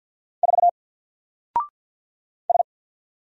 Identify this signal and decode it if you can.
Morse code, heard as a keyed tone, about 48 wpm, 700 Hz. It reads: 4 S
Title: Single Letter-Number Alt 50wpm